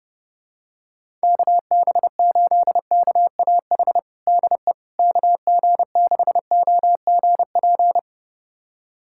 Morse code, heard as a keyed tone, about 32 wpm, 700 Hz. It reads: KB8KAH DE KG6OGP
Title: USA Callsign DE Callsign - 6 Characters - 30wpm